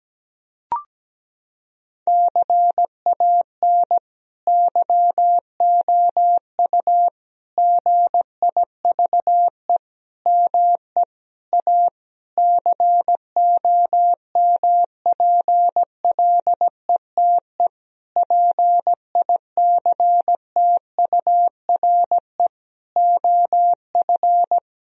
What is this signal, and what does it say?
Morse code, heard as a keyed tone, about 17 wpm, 700 Hz. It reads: CAN YOU GIVE ME A COMPLETE PICTURE OF
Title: Sentences from Top 1000 Words - Review Part 2 - 17wpm